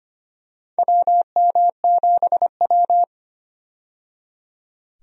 Morse code, heard as a keyed tone, about 25 wpm, 700 Hz. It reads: WM7W
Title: USA Call Signs - 4 Characters - 25wpm